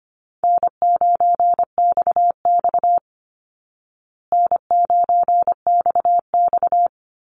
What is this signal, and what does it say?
Morse code, heard as a keyed tone, about 25 wpm, 700 Hz. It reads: N9XX N9XX
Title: Easy Call Signs Repeated 2x - 25wpm